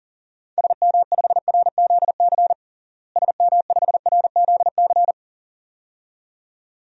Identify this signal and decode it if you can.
Morse code, heard as a keyed tone, about 40 wpm, 700 Hz. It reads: SM5RZC SM5RZC
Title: DXCC-100 Call Signs - 6 Characters - Repeated 2x - 40wpm